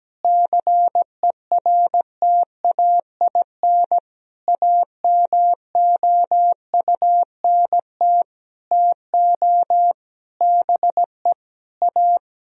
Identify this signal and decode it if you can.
Morse code, heard as a keyed tone, about 17 wpm, 700 Hz. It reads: CERTAIN AMOUNT TO BE A
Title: Sentences from Top 1000 Words - Review Part 4 - 17wpm